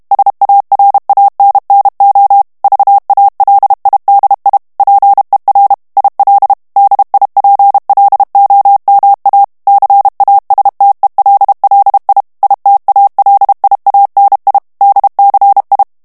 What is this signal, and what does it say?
Morse code, heard as a keyed tone, about 32 wpm, 800 Hz. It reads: SARANNO VALIDI PER IL DIPLOMA CASTELLI ITALIANI DCI
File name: ITALIEN texte clair 1 vitesse 30 mots minute